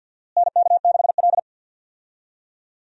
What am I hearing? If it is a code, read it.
Morse code, heard as a keyed tone, about 52 wpm, 700 Hz. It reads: NK6L
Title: USA Call Signs - 4 Characters - Repeated 2x - 50wpm